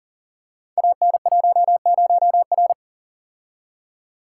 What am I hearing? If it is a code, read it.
Morse code, heard as a keyed tone, about 42 wpm, 700 Hz. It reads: AN10R
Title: Code Groups - 5 Letter:Numbers - 40wpm